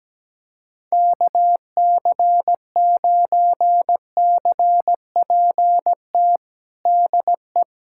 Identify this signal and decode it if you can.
Morse code, heard as a keyed tone, about 17 wpm, 700 Hz. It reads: KC9CPT DE
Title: USA Callsign DE Callsign - 6 Characters - 17wpm